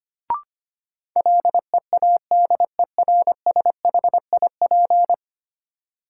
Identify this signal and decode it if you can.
Morse code, heard as a keyed tone, about 25 wpm, 700 Hz. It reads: LEADERSHIP
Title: Sentences from Top 1600 Words - New Word Drill - 25wpm